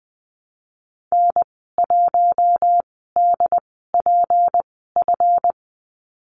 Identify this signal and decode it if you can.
Morse code, heard as a keyed tone, about 19 wpm, spaced as for 14 wpm, 700 Hz. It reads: N1DPF
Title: 154 Lesson 25g - Letter p - Call Signs 20wpm